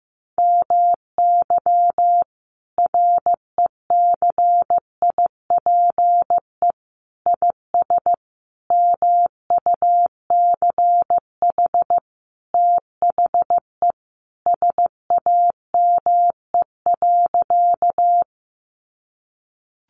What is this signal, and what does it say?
Morse code, heard as a keyed tone, about 15 wpm, 700 Hz. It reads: MY RECIPE IS MUCH THE SAME.
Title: Sentences from Top 1200 Words 15wpm